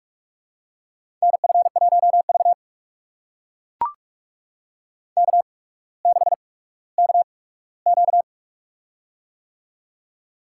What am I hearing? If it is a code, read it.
Morse code, heard as a keyed tone, about 44 wpm, spaced as for 19 wpm, 700 Hz. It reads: NF1V K6KQ
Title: USA Call Signs - 4 Characters - 8x Character Spacing - 45wpm